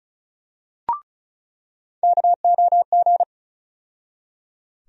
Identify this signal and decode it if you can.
Morse code, heard as a keyed tone, about 34 wpm, 700 Hz. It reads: KOG
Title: USA Call Sign Postfixes - 3 Letters 35wpm